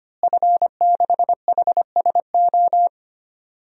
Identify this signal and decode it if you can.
Morse code, heard as a keyed tone, about 25 wpm, 700 Hz. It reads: F6HSO
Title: DXCC-100 Call Signs - 5 Characters - Repeated 3x - 25wpm